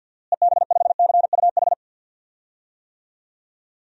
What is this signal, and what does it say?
Morse code, heard as a keyed tone, about 52 wpm, 700 Hz. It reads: EB5CUH
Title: DXCC-100 Call Signs - 6 Characters - Repeated 3x - 50wpm